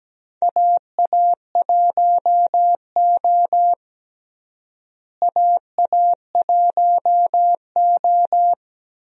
Morse code, heard as a keyed tone, about 17 wpm, 700 Hz. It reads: AA1O AA1O
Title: USA Call Signs - 4 Characters - Repeated 2x - 17wpm